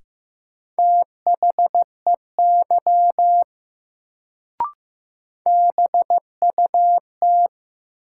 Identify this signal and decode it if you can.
Morse code, heard as a keyed tone, about 15 wpm, 700 Hz. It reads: THEY BUT
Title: Sentences from Top 100 Words - New Word Drill - 15wpm